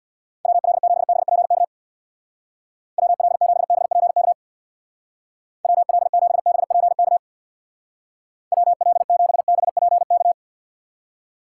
Morse code, heard as a keyed tone, about 66 wpm, 700 Hz. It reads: WL7BPK WL7BPK WL7BPK WL7BPK